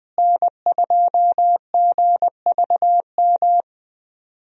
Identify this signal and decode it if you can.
Morse code, heard as a keyed tone, about 20 wpm, 700 Hz. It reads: N2GVM
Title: USA Call Signs - 5 Characters - Repeated 2x - 20wpm